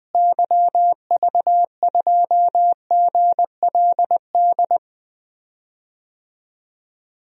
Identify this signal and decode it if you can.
Morse code, heard as a keyed tone, about 21 wpm, 700 Hz. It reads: YV2GLD
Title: DXCC-100 Call Signs - 6 Characters - Repeated 2x - 20wpm